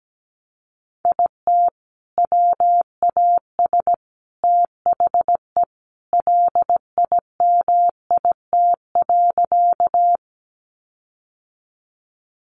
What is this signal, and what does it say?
Morse code, heard as a keyed tone, about 17 wpm, 700 Hz. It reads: IT WAS THE LIMIT.
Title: Sentences from Top 1500 Words 17wpm